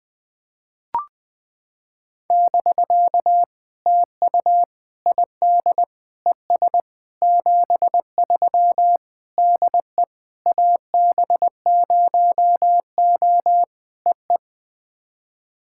Morse code, heard as a keyed tone, about 20 wpm, 700 Hz. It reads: <BK> TU ID ES 73 DE AB0O EE
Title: POTA QSO Practice - 20wpm